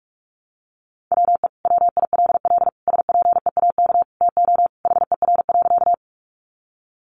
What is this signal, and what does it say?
Morse code, heard as a keyed tone, about 44 wpm, 700 Hz. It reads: WE WILL SPEAK TO HER.
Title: Sentences from Top 1100 Words 45wpm